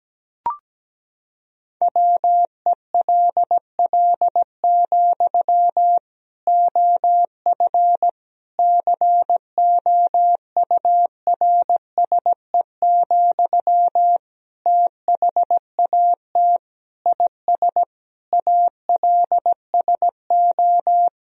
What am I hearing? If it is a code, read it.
Morse code, heard as a keyed tone, about 17 wpm, 700 Hz. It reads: WELL, OF COURSE, THAT IS ALSO
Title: Sentences from Top 1000 Words - Review Part 1 - 17wpm